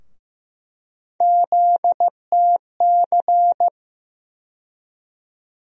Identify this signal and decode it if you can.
Morse code, heard as a keyed tone, about 15 wpm, 700 Hz. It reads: ZTC
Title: USA Call Sign Postfixes - 3 Letters 15wpm